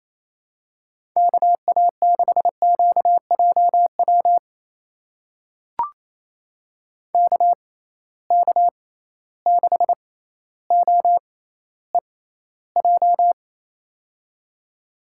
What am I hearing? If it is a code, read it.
Morse code, heard as a keyed tone, about 28 wpm, spaced as for 14 wpm, 700 Hz. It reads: KA6QJW KK6OEJ
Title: USA Call Signs - 6 Characters - 6x Character Spacing - 28wpm